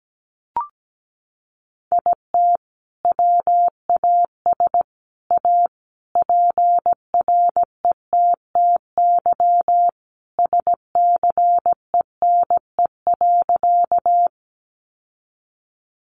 Morse code, heard as a keyed tone, about 17 wpm, 700 Hz. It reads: IT WAS A PRETTY SCENE.
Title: Sentences from Top 1100 Words 17wpm